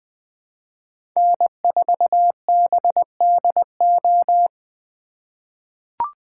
Morse code, heard as a keyed tone, about 20 wpm, 700 Hz. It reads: N4BDO
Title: USA Call Signs - 4x Character Spacing - 20wpm